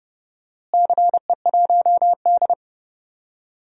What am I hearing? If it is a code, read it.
Morse code, heard as a keyed tone, about 30 wpm, 700 Hz. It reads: CE1D
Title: DXCC-100 Call Signs - 4 Characters - Repeated 2x - 30wpm